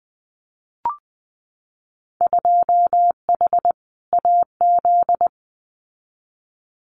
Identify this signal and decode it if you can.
Morse code, heard as a keyed tone, about 20 wpm, 700 Hz. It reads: 2H AZ
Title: Winter Field Day Exchanges 20wpm